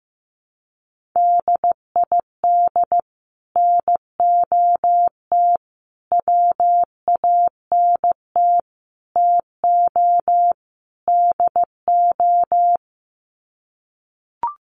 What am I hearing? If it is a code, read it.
Morse code, heard as a keyed tone, about 15 wpm, 700 Hz. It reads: DID NOT WANT TO DO
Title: Sets of 5 Words Encore 15wpm